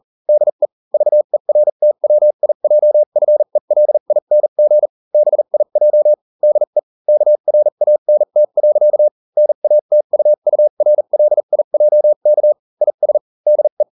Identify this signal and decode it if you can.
Morse code, heard as a keyed tone, about 40 wpm, spaced as for 35 wpm, 600 Hz. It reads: DE VERTWIJFELING BIJ DE KRANT. NATUURLIJK IS DE